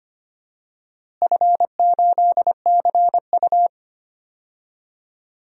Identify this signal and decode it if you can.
Morse code, heard as a keyed tone, about 25 wpm, 700 Hz. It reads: F8CU
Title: DXCC-100 Call Signs - 4 Characters - Repeated 2x - 25wpm